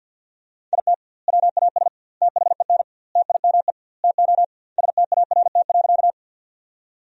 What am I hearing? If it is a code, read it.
Morse code, heard as a keyed tone, about 50 wpm, 700 Hz. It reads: IT WAS THEN TIME TO START.
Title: 2x Word-Spacing - Sentences from Top 300 Words - 50wpm